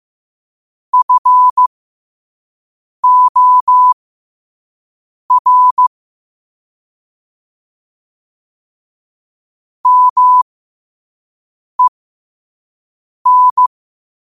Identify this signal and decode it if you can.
Morse code, heard as a keyed tone, about 15 wpm, spaced as for 5 wpm, 1000 Hz. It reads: FOR MEN
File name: qotdfiles / 5wpm